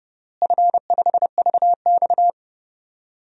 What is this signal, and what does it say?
Morse code, heard as a keyed tone, about 30 wpm, 700 Hz. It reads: F5VX
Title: DXCC-100 Call Signs - 4 Characters - Repeated 3x - 30wpm